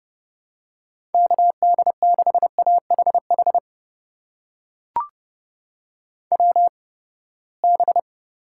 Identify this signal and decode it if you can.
Morse code, heard as a keyed tone, about 32 wpm, spaced as for 21 wpm, 700 Hz. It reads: KD6AHH WB
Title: USA Call Signs - 6 Characters - 8x Character Spacing - 30wpm